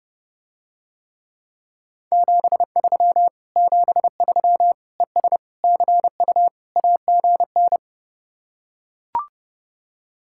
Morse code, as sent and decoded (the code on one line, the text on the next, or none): --... ...-- / --... ...-- / . ... / -.-. ..- / .- --. -.
73 73 ES CU AGN